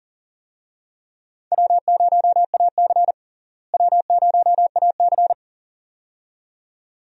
Morse code, heard as a keyed tone, about 40 wpm, 700 Hz. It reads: W0AC W0AC
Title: USA Call Signs - 4 Characters - Repeated 2x - 40wpm